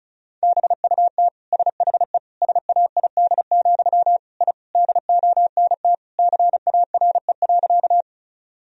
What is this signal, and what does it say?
Morse code, heard as a keyed tone, about 34 wpm, 700 Hz. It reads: BUT SHE SAID, I DONT CARE.
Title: Sentences from Top 1000 Words - Review Part 1 - 35wpm